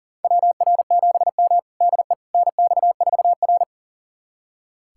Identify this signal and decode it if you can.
Morse code, heard as a keyed tone, about 40 wpm, 700 Hz. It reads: WR7M DE NX4R
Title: USA Callsign DE Callsign - 4 Characters - 40wpm